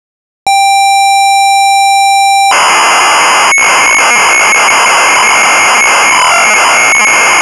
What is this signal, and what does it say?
ZX Spectrum tape loading.
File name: Sinclair Zx Spectrum / Robin of the Wood (Unknown)